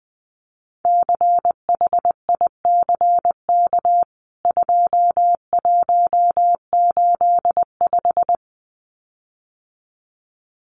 Morse code, heard as a keyed tone, about 20 wpm, 700 Hz. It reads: CHICK 2185
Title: CWT Exchange 20wpm